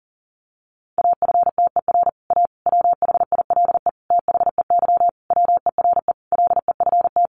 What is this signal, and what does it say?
Morse code, heard as a keyed tone, about 40 wpm, 700 Hz. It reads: AFTER A WHILE THEY WERE LEFT
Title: Sentences from Top 1000 Words 40wpm